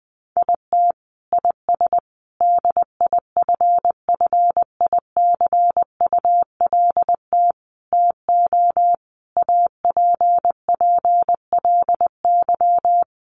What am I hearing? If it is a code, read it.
Morse code, heard as a keyed tone, about 20 wpm, 700 Hz. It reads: IT IS DIFFICULT TO APPLY
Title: Sentences from Top 1200 Words 20wpm